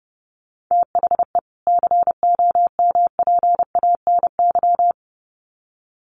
Morse code, heard as a keyed tone, about 30 wpm, 700 Hz. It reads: THE COMPANY
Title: Sets of 2 Words 30wpm